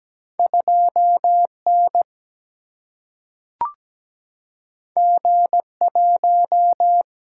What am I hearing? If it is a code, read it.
Morse code, heard as a keyed tone, about 17 wpm, 700 Hz. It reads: 2N G1
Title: Single-Letter Single-Number 17wpm